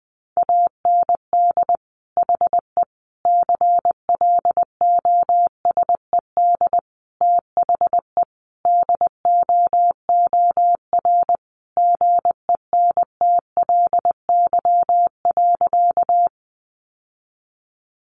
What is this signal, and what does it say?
Morse code, heard as a keyed tone, about 20 wpm, 700 Hz. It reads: AND HE CLOSED THE DOOR GENTLY.
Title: Sentences from Top 1500 Words 20wpm